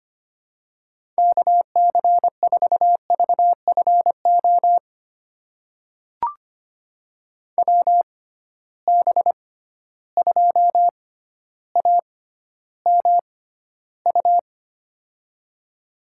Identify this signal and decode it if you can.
Morse code, heard as a keyed tone, about 25 wpm, spaced as for 13 wpm, 700 Hz. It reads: KC4VFO WB2AMU
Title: USA Call Signs - 6 Characters - 6x Character Spacing - 25wpm